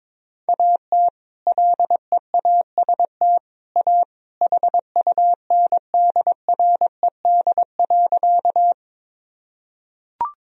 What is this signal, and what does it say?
Morse code, heard as a keyed tone, about 22 wpm, 700 Hz. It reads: AT LEAST A HUNDRED.
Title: Sentences from Top 1000 Words - Review Part 2 - 22wpm